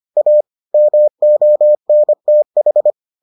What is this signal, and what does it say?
Morse code, heard as a keyed tone, about 25 wpm, 600 Hz. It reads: A MONTH